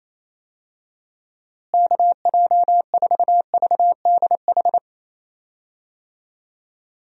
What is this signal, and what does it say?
Morse code, heard as a keyed tone, about 28 wpm, 700 Hz. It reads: KJ4VDH
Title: USA Call Signs - 6 Characters - Repeated 2x - 28wpm